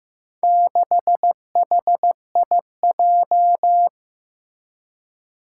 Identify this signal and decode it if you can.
Morse code, heard as a keyed tone, about 15 wpm, 700 Hz. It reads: 6HIJ
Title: Four Challenging Letter-Number Combinations 15wpm